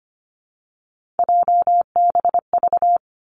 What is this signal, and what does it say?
Morse code, heard as a keyed tone, about 25 wpm, 700 Hz. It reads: JBV
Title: Three Challenging Letter-Number Combinations 25wpm